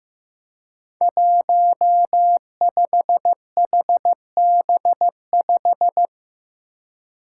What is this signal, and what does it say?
Morse code, heard as a keyed tone, about 15 wpm, 700 Hz. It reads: 15HB5
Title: Five Challenging Letter-Number Combinations 15wpm